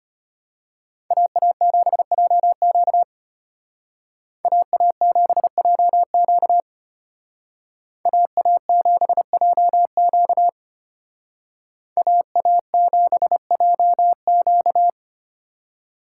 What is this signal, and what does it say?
Morse code, heard as a keyed tone, about 32 wpm, 700 Hz. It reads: AA7JQ AA7JQ AA7JQ AA7JQ